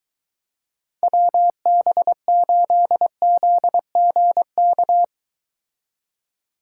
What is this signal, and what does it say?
Morse code, heard as a keyed tone, about 22 wpm, 700 Hz. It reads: WB8ZGK